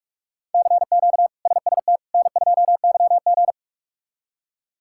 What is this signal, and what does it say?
Morse code, heard as a keyed tone, about 41 wpm, 700 Hz. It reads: CQ SST N2YG
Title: K1USN SST QSO Practice - 45wpm